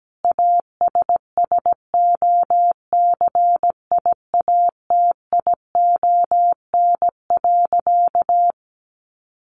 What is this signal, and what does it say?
Morse code, heard as a keyed tone, about 17 wpm, 700 Hz. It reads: ASSOCIATION.
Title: Sentences from Top 1200 Words 17wpm